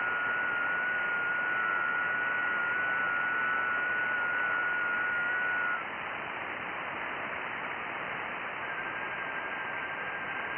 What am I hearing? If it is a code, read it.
FT8.
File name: IC-7100